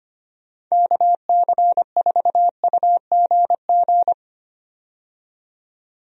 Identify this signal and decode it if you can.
Morse code, heard as a keyed tone, about 25 wpm, 700 Hz. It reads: KC4UGG
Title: USA Call Signs - 6 Characters - 25wpm